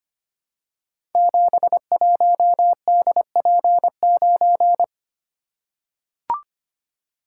Morse code, heard as a keyed tone, about 25 wpm, 700 Hz. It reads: 71DP9
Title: Code Groups - 5 Letter:Numbers - 25wpm